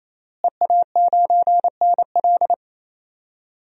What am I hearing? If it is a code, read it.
Morse code, heard as a keyed tone, about 28 wpm, 700 Hz. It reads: EA9NL
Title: DXCC-100 Call Signs - 5 Characters - Repeated 3x - 28wpm